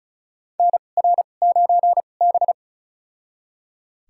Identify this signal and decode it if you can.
Morse code, heard as a keyed tone, about 34 wpm, spaced as for 25 wpm, 700 Hz. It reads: NR9B
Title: USA Call Signs - 4 Characters - 2x Character Spacing - 35wpm